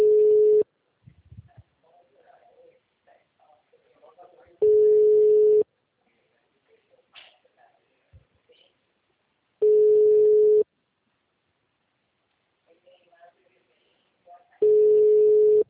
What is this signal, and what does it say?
Ringback tone (Europe, 425 Hz), with speech.